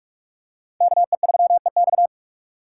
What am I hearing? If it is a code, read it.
Morse code, heard as a keyed tone, about 40 wpm, 700 Hz. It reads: KE3EX